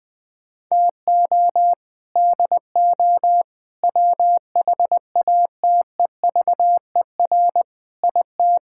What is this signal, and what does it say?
Morse code, heard as a keyed tone, about 20 wpm, 700 Hz. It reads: TO DO WHATEVER IT
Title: Sets of 6 Words - 20wpm